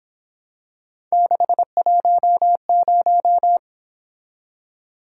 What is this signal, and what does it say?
Morse code, heard as a keyed tone, about 26 wpm, 700 Hz. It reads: 610
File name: Speed-Racing - Three Letter-Number Combinations - 22wpm